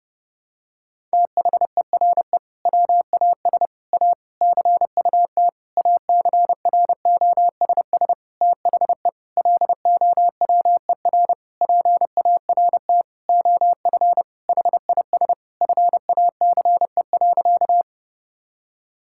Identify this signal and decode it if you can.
Morse code, heard as a keyed tone, about 30 wpm, 700 Hz. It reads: THERE WAS A CUT ACROSS THE LOWER PART OF HIS FACE.
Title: Sentences from Top 1000 Words - Review Part 5 - 30wpm